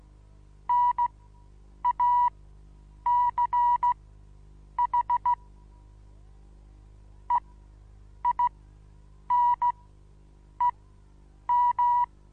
Morse code, heard as a keyed tone, about 16 wpm, spaced as for 8 wpm, 970 Hz. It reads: NACH EINEM
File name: Onlinelehrgang / morsen / DJ4UF Morsen25